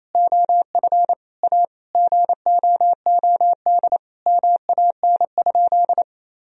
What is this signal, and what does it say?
Morse code, heard as a keyed tone, about 28 wpm, 700 Hz. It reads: OF A GOOD MAN?
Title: Sentences from Top 1000 Words - Review Part 4 - 28wpm